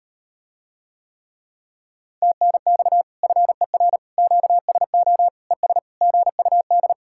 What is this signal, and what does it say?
Morse code, heard as a keyed tone, about 39 wpm, 700 Hz. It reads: TNX FER QSO ES GUD